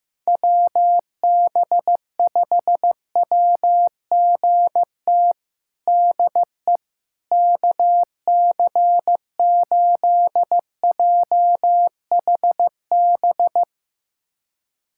Morse code, heard as a keyed tone, about 15 wpm, 700 Hz. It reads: WB5WGT DE KC8JHB
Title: USA Callsign DE Callsign 15wpm